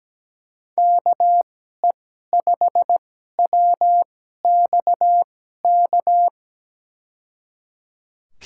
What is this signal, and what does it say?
Morse code, heard as a keyed tone, about 17 wpm, spaced as for 12 wpm, 700 Hz. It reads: KE5WXK
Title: USA Call Signs - 6 Characters - 2x Character Spacing - 17wpm